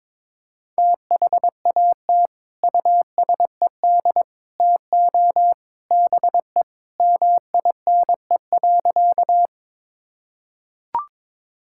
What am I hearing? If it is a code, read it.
Morse code, heard as a keyed tone, about 22 wpm, 700 Hz. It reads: THAT USED TO BE MINE.
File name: practice / 22wpm / Sentences with 3-Word Phrases - 22wpm